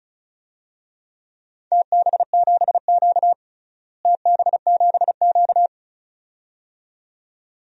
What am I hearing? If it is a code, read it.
Morse code, heard as a keyed tone, about 34 wpm, 700 Hz. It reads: TB7Q TB7Q
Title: DXCC-100 Call Signs - 4 Characters - Repeated 2x - 35wpm